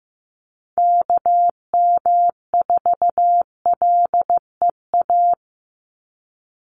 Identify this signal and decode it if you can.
Morse code, heard as a keyed tone, about 15 wpm, 700 Hz. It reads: KM4LEA
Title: USA Call Signs 15wpm